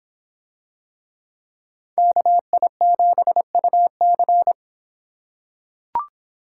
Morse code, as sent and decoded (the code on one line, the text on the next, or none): -.- .. --... ..- -.-.
KI7UC